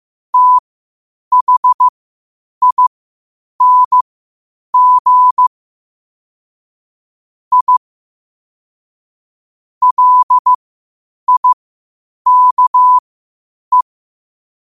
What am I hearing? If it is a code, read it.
Morse code, heard as a keyed tone, about 15 wpm, spaced as for 8 wpm, 1000 Hz. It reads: THING I LIKE
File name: qotdfiles / 8wpm